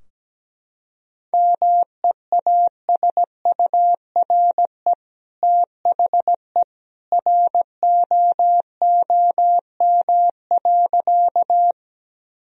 Morse code, heard as a keyed tone, about 17 wpm, 700 Hz. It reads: MEASURE THE ROOM.
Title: Sentences from Top 1000 Words - Review Part 2 - 17wpm